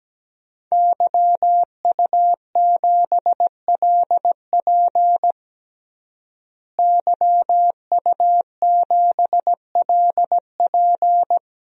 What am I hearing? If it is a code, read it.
Morse code, heard as a keyed tone, about 17 wpm, 700 Hz. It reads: YU7LP YU7LP
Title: DXCC-100 Call Signs - 5 Characters - Repeated 2x - 17wpm